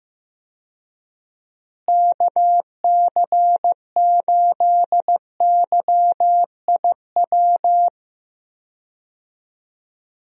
Morse code, heard as a keyed tone, about 15 wpm, 700 Hz. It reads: KC8YIW
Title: USA Call Signs - 6 Characters - Repeated 3x - 15wpm